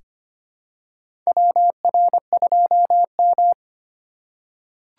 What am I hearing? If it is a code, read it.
Morse code, heard as a keyed tone, about 25 wpm, 700 Hz. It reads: WR2M
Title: USA Call Signs - 4 Characters - 4x Character Spacing - 25wpm